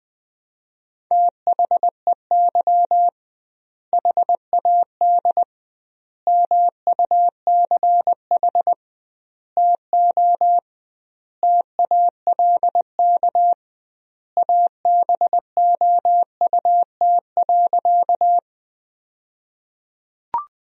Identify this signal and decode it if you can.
Morse code, heard as a keyed tone, about 21 wpm, 700 Hz. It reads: THEY HAD MUCH TO TALK ABOUT.
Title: 2x Word-Spacing - Sentences from Top 400 Words - 20wpm